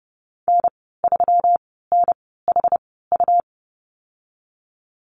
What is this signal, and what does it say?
Morse code, heard as a keyed tone, about 30 wpm, spaced as for 17 wpm, 700 Hz. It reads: N3NHU
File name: Lesson 37g - Letter x - Call Signs 30wpm